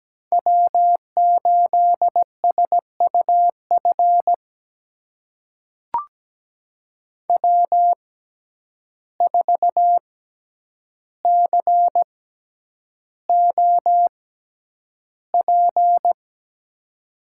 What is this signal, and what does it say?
Morse code, heard as a keyed tone, about 17 wpm, spaced as for 9 wpm, 700 Hz. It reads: W8SUF W4COP
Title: USA Call Signs - 5 Characters - 6x Character Spacing - 17wpm